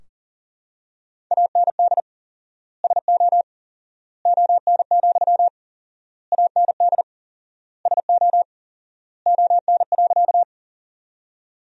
Morse code, heard as a keyed tone, about 40 wpm, 700 Hz. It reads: AND SO ON, AND SO ON.
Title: 4x Word-Spacing - Sentences from Top 100 Words - 40wpm